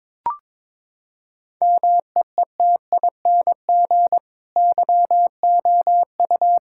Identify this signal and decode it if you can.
Morse code, heard as a keyed tone, about 21 wpm, 700 Hz. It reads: MEETING YOU
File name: practice / 22wpm / Sentences with 8-Word Phrases - 22wpm